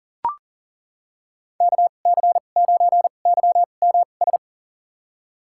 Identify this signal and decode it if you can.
Morse code, heard as a keyed tone, about 40 wpm, spaced as for 29 wpm, 700 Hz. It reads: KC9YMS
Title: USA Call Signs - 2x Character Spacing - 40wpm